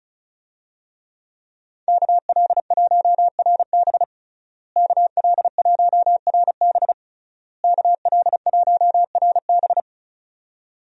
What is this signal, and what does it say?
Morse code, heard as a keyed tone, about 36 wpm, 700 Hz. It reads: KL1RB KL1RB KL1RB
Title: USA Call Signs - 5 Characters - Repeated 3x - 35wpm